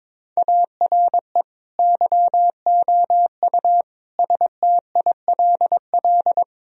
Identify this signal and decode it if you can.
Morse code, heard as a keyed tone, about 21 wpm, 700 Hz. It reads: ARE YOU STILL
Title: Sentences from Top 1000 Words - Review Part 2 - 22wpm